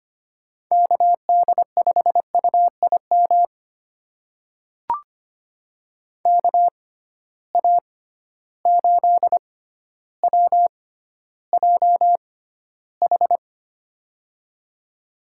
Morse code, heard as a keyed tone, about 25 wpm, spaced as for 13 wpm, 700 Hz. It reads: KD5UIM KA8WJH
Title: USA Call Signs - 6 Characters - 6x Character Spacing - 25wpm